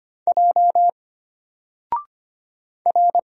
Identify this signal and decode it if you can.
Morse code, heard as a keyed tone, about 25 wpm, 700 Hz. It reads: J R
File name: Single-Letter-Number-25wpm